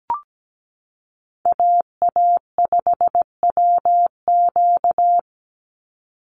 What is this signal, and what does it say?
Morse code, heard as a keyed tone, about 17 wpm, 700 Hz. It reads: AA5WQ
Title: USA Call Signs 17wpm